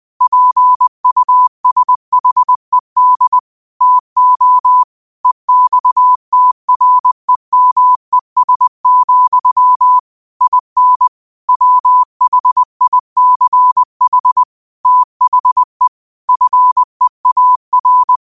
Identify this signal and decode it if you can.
Morse code, heard as a keyed tone, about 20 wpm, 1000 Hz. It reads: PUSHED TO EXTREMES, IN WHICH THE FEAR
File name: qotdfiles / 20wpm